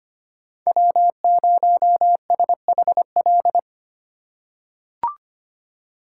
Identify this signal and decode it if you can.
Morse code, heard as a keyed tone, about 25 wpm, 700 Hz. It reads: W0SHL
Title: K1USN SST QSO Practice - 25wpm